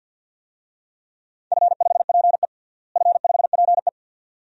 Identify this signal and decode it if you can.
Morse code, heard as a keyed tone, about 51 wpm, 700 Hz. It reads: F5PE F5PE
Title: DXCC-100 Call Signs - 4 Characters - Repeated 2x - 50wpm